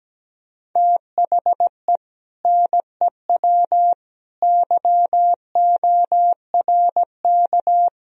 Morse code, heard as a keyed tone, about 17 wpm, 700 Hz. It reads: THE NEW YORK
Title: Speed-Racing - Sets of 5 Words 17wpm